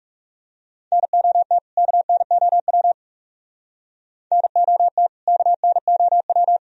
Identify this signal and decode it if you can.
Morse code, heard as a keyed tone, about 37 wpm, 700 Hz. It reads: NOT KNOW NOT KNOW
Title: Speed-Racing - Sets of 2 Words 30wpm